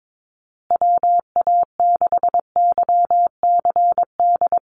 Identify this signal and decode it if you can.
Morse code, heard as a keyed tone, about 22 wpm, 700 Hz. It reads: WA6YCD
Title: USA Call Signs Repeated 2x 22wpm